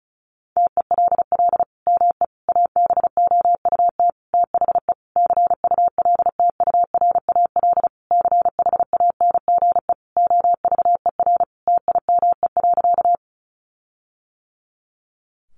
Morse code, heard as a keyed tone, about 34 wpm, 700 Hz. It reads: TELL ME ABOUT THE CULTURAL CHANGE OVER TIME.
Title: Sentences from Top 1500 Words 35wpm